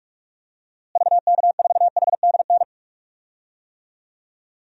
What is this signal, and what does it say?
Morse code, heard as a keyed tone, about 46 wpm, 700 Hz. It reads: VK4HDN
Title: DXCC-100 Call Signs - 6 Characters - Repeated 3x - 45wpm